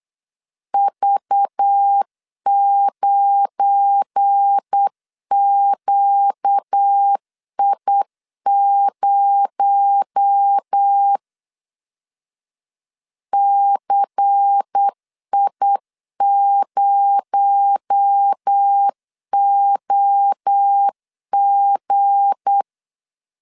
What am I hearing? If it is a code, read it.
Morse code, heard as a keyed tone, about 8 wpm, 800 Hz. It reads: V9QI0 CI0OG
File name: LEZIONI / 7 GF9Z / 7 GF9Z 8wpm 3213